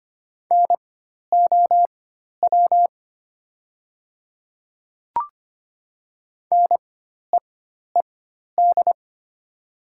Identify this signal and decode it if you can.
Morse code, heard as a keyed tone, about 25 wpm, spaced as for 12 wpm, 700 Hz. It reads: NOW NEED
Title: 4x Character Spacing - Top 200 Words - 25wpm